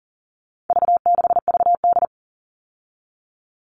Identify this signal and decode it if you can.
Morse code, heard as a keyed tone, about 40 wpm, 700 Hz. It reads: V6VD
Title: All Cut Numbers - 4-Digit 40wpm